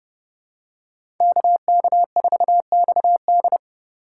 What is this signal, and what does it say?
Morse code, heard as a keyed tone, about 28 wpm, 700 Hz. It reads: KK4XD